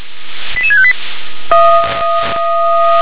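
Five-tone selective call (ZVEI), heard as quick runs of five tones, tones of about 70 ms. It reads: 88658
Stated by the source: ZVEI five-tone selective call received on an SDR.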